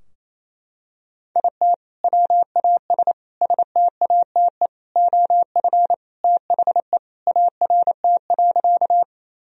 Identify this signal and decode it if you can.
Morse code, heard as a keyed tone, about 28 wpm, 700 Hz. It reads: IT WAS STATE OF THE ART.
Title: Sentences from Top 1000 Words - Review Part 3 - 28wpm